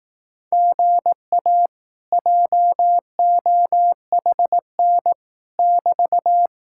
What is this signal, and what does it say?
Morse code, heard as a keyed tone, about 18 wpm, 700 Hz. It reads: GA JOHN <BT>
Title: Speed-Racing Common QSO Phrases Encore - 15wpm